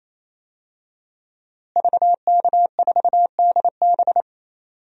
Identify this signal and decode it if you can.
Morse code, heard as a keyed tone, about 28 wpm, 700 Hz. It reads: VK4DB
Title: DXCC-100 Call Signs - 5 Characters - Repeated 3x - 28wpm